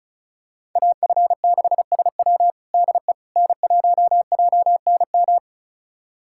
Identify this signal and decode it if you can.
Morse code, heard as a keyed tone, about 36 wpm, 700 Hz. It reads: AF6SW DE N1JNM
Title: USA Callsign DE Callsign - 5 Characters - 35wpm